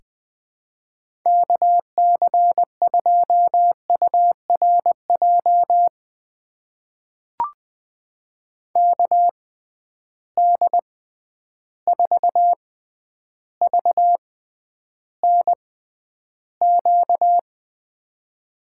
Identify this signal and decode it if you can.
Morse code, heard as a keyed tone, about 21 wpm, spaced as for 10 wpm, 700 Hz. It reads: KC2URJ KD4VNQ
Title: USA Call Signs - 6 Characters - 6x Character Spacing - 20wpm